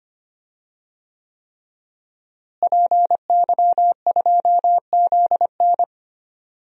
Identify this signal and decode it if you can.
Morse code, heard as a keyed tone, about 25 wpm, 700 Hz. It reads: PY2ZN
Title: DXCC-100 Call Signs - 5 Characters - Repeated 3x - 25wpm